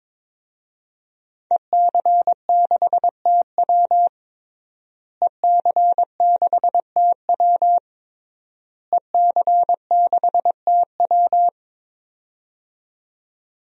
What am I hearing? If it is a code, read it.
Morse code, heard as a keyed tone, about 22 wpm, 700 Hz. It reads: EC6TW EC6TW EC6TW
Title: DXCC-100 Call Signs - 5 Characters - Repeated 3x - 22wpm